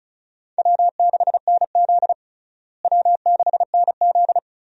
Morse code, heard as a keyed tone, about 34 wpm, 700 Hz. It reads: W6NZ W6NZ
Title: USA Call Signs - 4 Characters - Repeated 2x - 35wpm